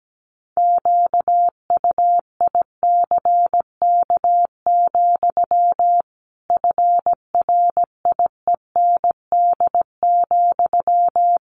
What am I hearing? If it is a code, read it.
Morse code, heard as a keyed tone, about 17 wpm, 700 Hz. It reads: QUICK, FRIEND,
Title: Sentences from Top 500 Words 17wpm